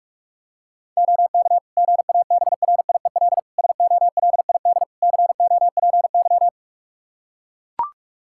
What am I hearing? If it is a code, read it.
Morse code, heard as a keyed tone, about 44 wpm, 700 Hz. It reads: OK GABRIEL SOLID COPY
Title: Practice QSO - Exchange 3 - More Info - 45wpm